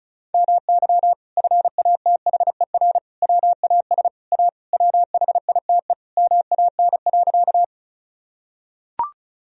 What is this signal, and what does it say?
Morse code, heard as a keyed tone, about 34 wpm, 700 Hz. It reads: MY FATHER WAS A WHITE MAN.
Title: Sentences from Top 1000 Words - Review Part 1 - 35wpm